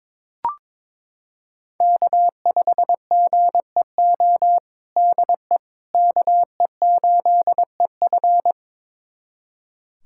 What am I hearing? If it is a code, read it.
Morse code, heard as a keyed tone, about 22 wpm, 700 Hz. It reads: K5GEO DE KE8EF
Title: USA Callsign DE Callsign - 5 Characters - 22wpm